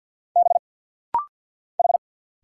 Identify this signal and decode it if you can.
Morse code, heard as a keyed tone, about 50 wpm, 690 Hz. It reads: B H
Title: Single Letter-Number Alt - Rapid-Fire 50wpm